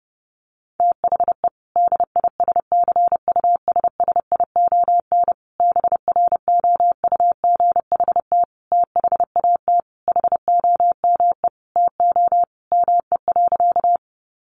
Morse code, heard as a keyed tone, about 30 wpm, 700 Hz. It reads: THE DISCUSSION BROUGHT THAT HOME TO ME.
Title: Sentences from Top 1100 Words 30wpm